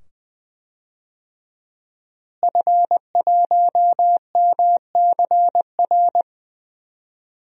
Morse code, heard as a keyed tone, about 20 wpm, 700 Hz. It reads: F1MCR
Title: DXCC-100 Call Signs - 5 Characters - Repeated 2x - 20wpm